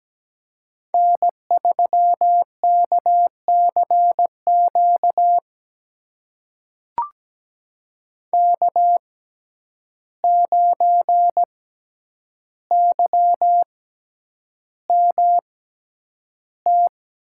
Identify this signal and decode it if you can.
Morse code, heard as a keyed tone, about 17 wpm, spaced as for 9 wpm, 700 Hz. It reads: N3KCQ K9YMT
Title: USA Call Signs - 5 Characters - 6x Character Spacing - 17wpm